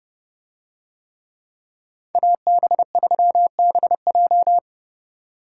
Morse code, heard as a keyed tone, about 30 wpm, 700 Hz. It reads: AB3BJ